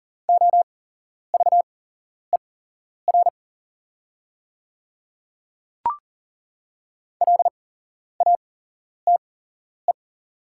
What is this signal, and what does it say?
Morse code, heard as a keyed tone, about 39 wpm, spaced as for 11 wpm, 690 Hz. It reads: OVER LATE
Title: 8x Character Spacing - Top 300 Words - 40wpm